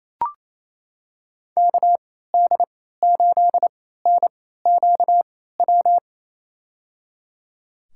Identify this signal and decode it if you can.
Morse code, heard as a keyed tone, about 28 wpm, spaced as for 16 wpm, 700 Hz. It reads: KD8NQW
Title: USA Call Signs - 6 Characters - 3x Character Spacing - 28wpm